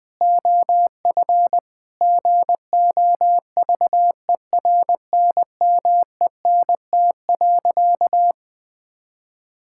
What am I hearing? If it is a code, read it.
Morse code, heard as a keyed tone, about 20 wpm, 700 Hz. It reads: OF GOVERNMENT.
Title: Sentences from Top 1000 Words - Review Part 6 - 20wpm